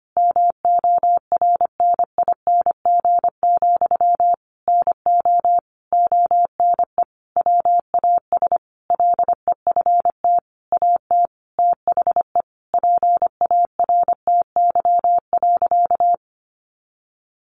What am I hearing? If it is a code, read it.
Morse code, heard as a keyed tone, about 25 wpm, 700 Hz. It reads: MORNING, NO ONE WAS LEFT AT THE PARTY.
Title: Sentences from Top 1400 Words 25wpm